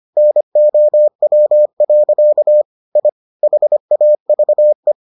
Morse code, heard as a keyed tone, about 25 wpm, 600 Hz. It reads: NOW. I HAVE